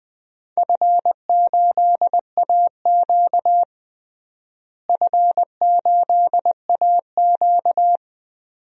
Morse code, heard as a keyed tone, about 20 wpm, 700 Hz. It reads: F8AQ F8AQ
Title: DXCC-100 Call Signs - 4 Characters - Repeated 2x - 20wpm